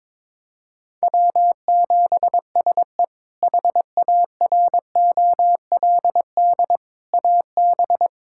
Morse code, heard as a keyed tone, about 22 wpm, 700 Hz. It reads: W7SE HAROLD AB
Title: K1USN SST QSO Practice - 22wpm